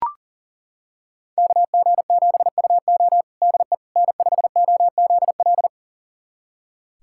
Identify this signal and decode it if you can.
Morse code, heard as a keyed tone, about 40 wpm, 700 Hz. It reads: KG7UO DE N5OZL
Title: USA Callsign DE Callsign - 5 Characters - 40wpm